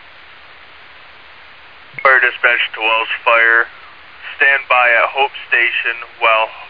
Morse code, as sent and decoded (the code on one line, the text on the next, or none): none